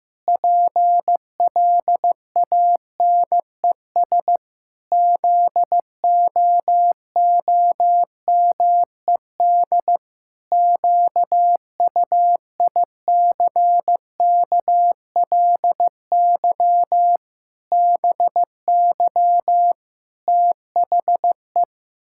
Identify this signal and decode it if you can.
Morse code, heard as a keyed tone, about 16 wpm, 700 Hz. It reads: PLANES ZOOMED QUICKLY BY THE
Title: Pangram Challenge v2 - 15wpm